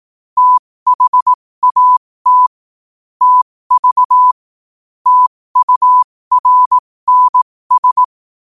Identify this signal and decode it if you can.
Morse code, heard as a keyed tone, about 18 wpm, spaced as for 15 wpm, 1000 Hz. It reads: THAT TV TURNS
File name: qotdfiles / 15wpm